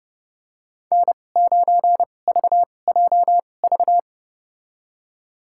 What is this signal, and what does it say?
Morse code, heard as a keyed tone, about 30 wpm, spaced as for 22 wpm, 700 Hz. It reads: N9VJV
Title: USA Call Signs - 5 Characters - 2x Character Spacing - 30wpm